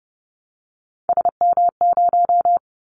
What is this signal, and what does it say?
Morse code, heard as a keyed tone, about 30 wpm, 700 Hz. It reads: SM0
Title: Callsign Prefixes for Top 25 DXCC - 30wpm